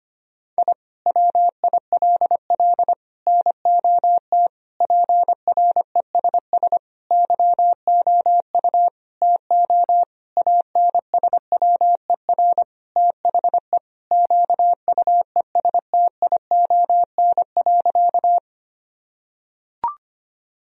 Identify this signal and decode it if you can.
Morse code, heard as a keyed tone, about 25 wpm, 700 Hz. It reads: I WILL NOT PRESS YOU TO ANSWER THE QUESTION.
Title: Sentences from Top 1000 Words - Review Part 1 - 25wpm